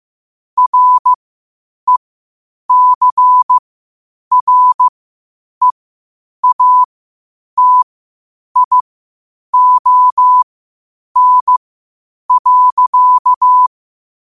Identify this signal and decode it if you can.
Morse code, heard as a keyed tone, about 15 wpm, spaced as for 8 wpm, 1000 Hz. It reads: RECREATION.
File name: qotdfiles / 8wpm